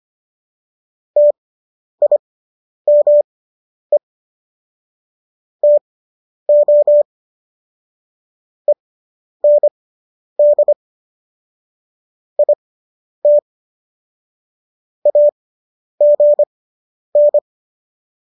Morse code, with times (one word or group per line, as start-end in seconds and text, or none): 1.2-4.0 TIME
5.6-7.0 TO
8.7-10.7 END
12.4-13.4 IT
15.1-17.4 AGN